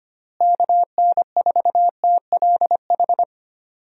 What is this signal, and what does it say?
Morse code, heard as a keyed tone, about 25 wpm, 700 Hz. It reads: KN4TLH
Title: USA Call Signs - 6 Characters - Repeated 3x - 25wpm